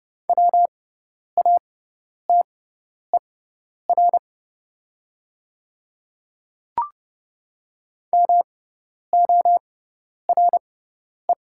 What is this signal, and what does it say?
Morse code, heard as a keyed tone, about 30 wpm, spaced as for 10 wpm, 690 Hz. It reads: WATER MORE
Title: 6x Character Spacing - Top 500 Words - 30wpm